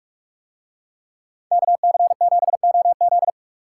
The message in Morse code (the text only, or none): KC7OZ